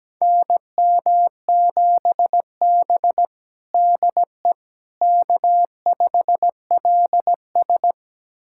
Morse code, heard as a keyed tone, about 17 wpm, 700 Hz. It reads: NM7B DE K5LS
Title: USA Callsign DE Callsign - 4 Characters - 17wpm